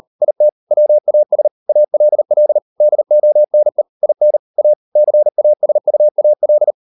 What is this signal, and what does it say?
Morse code, heard as a keyed tone, about 38 wpm, 600 Hz. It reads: IT WAS ALL DONE IN A CASUAL